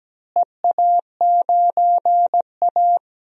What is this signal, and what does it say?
Morse code, heard as a keyed tone, about 17 wpm, 700 Hz. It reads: EA9A
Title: DXCC-100 Call Signs - 4 Characters - Repeated 2x - 17wpm